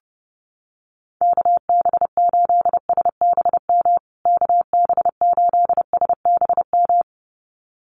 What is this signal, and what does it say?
Morse code, heard as a keyed tone, about 30 wpm, 700 Hz. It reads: KB8SBM KB8SBM
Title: USA Call Signs Repeated 2x 30wpm